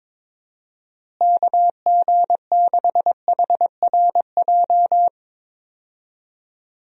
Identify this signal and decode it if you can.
Morse code, heard as a keyed tone, about 22 wpm, 700 Hz. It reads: KG6HRJ
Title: USA Call Signs - 6 Characters - Repeated 3x - 22wpm